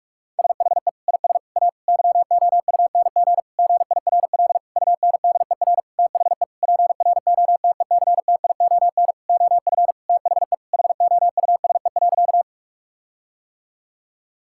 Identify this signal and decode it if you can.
Morse code, heard as a keyed tone, about 46 wpm, 700 Hz. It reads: SHE IS A YOUNG GIRL UNDER THE PROTECTION OF THE HOUSE.
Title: Sentences from Top 1600 Words 45wpm